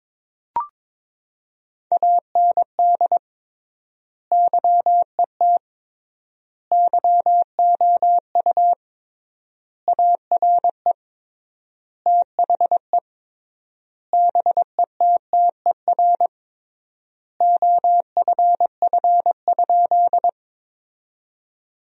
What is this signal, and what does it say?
Morse code, heard as a keyed tone, about 22 wpm, 700 Hz. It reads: AND YET YOU ARE THE BETTER OFF?
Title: 3x Word-Spacing - Sentences from Top 500 Words - 22wpm